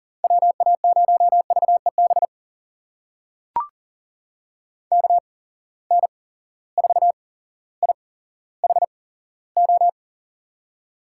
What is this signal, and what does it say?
Morse code, heard as a keyed tone, about 40 wpm, spaced as for 17 wpm, 700 Hz. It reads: WA0VEB KN4IHO
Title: USA Call Signs - 6 Characters - 8x Character Spacing - 40wpm